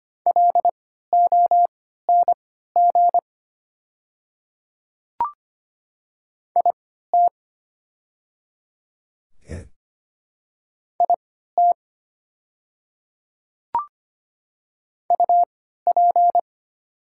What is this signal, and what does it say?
Morse code, heard as a keyed tone, about 25 wpm, spaced as for 14 wpm, 700 Hz. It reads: LONG IT IT UP
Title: 3x Character Spacing - Top 100 Words - 25wpm